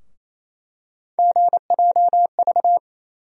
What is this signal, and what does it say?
Morse code, heard as a keyed tone, about 28 wpm, 700 Hz. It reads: GJV
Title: USA Call Sign Postfixes - 3 Letters 28wpm